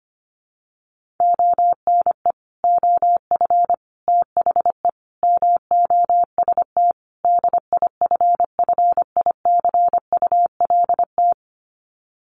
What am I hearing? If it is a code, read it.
Morse code, heard as a keyed tone, about 25 wpm, 700 Hz. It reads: ONE OF THE MOST DIFFICULT
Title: Sets of 5 Words Encore2 25wpm